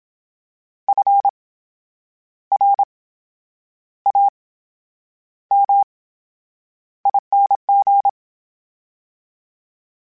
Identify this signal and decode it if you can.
Morse code, heard as a keyed tone, about 27 wpm, spaced as for 9 wpm, 800 Hz. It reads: FRAMING
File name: Special INGs / Special INGs-mp3s / VerySlowToFaster Ing @25